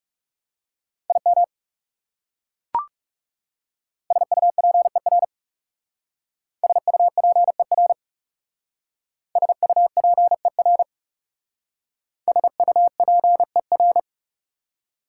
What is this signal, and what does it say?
Morse code, heard as a keyed tone, about 38 wpm, 690 Hz. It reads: IM SUPER SUPER SUPER SUPER